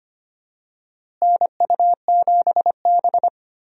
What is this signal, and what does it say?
Morse code, heard as a keyed tone, about 25 wpm, 700 Hz. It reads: NU7B
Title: USA Call Signs - 4 Characters - Repeated 3x - 25wpm